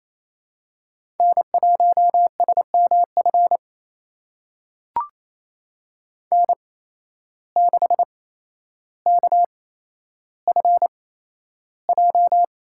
Morse code, heard as a keyed tone, about 28 wpm, spaced as for 12 wpm, 700 Hz. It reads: N1SMF N6KFJ
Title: USA Call Signs - 5 Characters - 8x Character Spacing - 28wpm